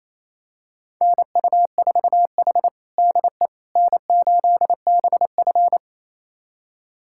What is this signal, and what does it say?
Morse code, heard as a keyed tone, about 28 wpm, 700 Hz. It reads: NU4H DE N8BF
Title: USA Callsign DE Callsign - 4 Characters - 28wpm